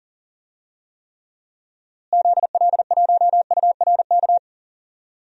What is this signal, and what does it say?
Morse code, heard as a keyed tone, about 40 wpm, 700 Hz. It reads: ZL1URK
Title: DXCC-100 Call Signs - 6 Characters - Repeated 3x - 40wpm